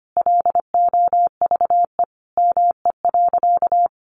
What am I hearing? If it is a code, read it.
Morse code, heard as a keyed tone, about 25 wpm, 700 Hz. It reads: LOVE ME.
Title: Sentences from Top 400 Words 25wpm